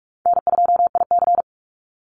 Morse code, heard as a keyed tone, about 44 wpm, 700 Hz. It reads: N2IC
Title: Call Signs Repeated 3x - 45wpm